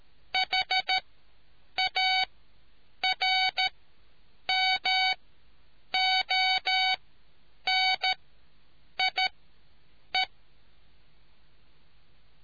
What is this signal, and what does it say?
Morse code, heard as a keyed tone, about 13 wpm, spaced as for 8 wpm, 750 Hz. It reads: HARMONIE